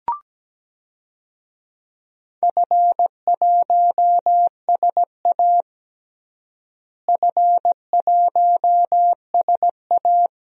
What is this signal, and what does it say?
Morse code, heard as a keyed tone, about 17 wpm, 700 Hz. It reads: F1SA F1SA
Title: DXCC-100 Call Signs - 4 Characters - Repeated 2x - 17wpm